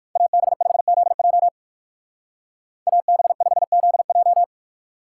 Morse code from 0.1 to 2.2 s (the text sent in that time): AB5ZJ